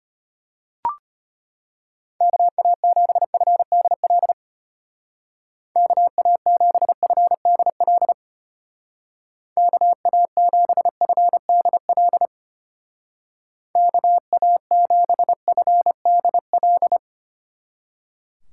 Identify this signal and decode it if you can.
Morse code, heard as a keyed tone, about 30 wpm, 700 Hz. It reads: KA7FDL KA7FDL KA7FDL KA7FDL
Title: Speed Racing - USA Call Signs 25wpm